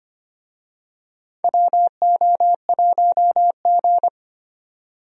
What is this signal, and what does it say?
Morse code, heard as a keyed tone, about 25 wpm, 700 Hz. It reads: WO1G
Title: USA Call Signs - 4 Characters - 25wpm